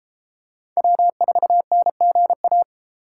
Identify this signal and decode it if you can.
Morse code, heard as a keyed tone, about 33 wpm, 700 Hz. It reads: W4NGA